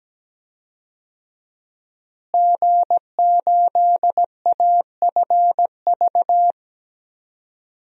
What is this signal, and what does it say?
Morse code, heard as a keyed tone, about 17 wpm, 700 Hz. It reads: G8AFV
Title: DXCC-100 Call Signs - 5 Characters - Repeated 2x - 17wpm